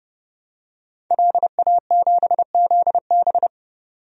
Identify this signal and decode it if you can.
Morse code, heard as a keyed tone, about 29 wpm, 690 Hz. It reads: LA7ZB
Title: DXCC-100 Call Signs - 5 Characters - 30wpm